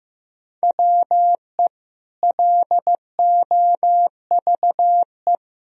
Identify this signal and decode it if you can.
Morse code, heard as a keyed tone, about 16 wpm, 700 Hz. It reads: WE LOVE
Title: Sentences from Top 1000 Words - Review Part 1 - 15wpm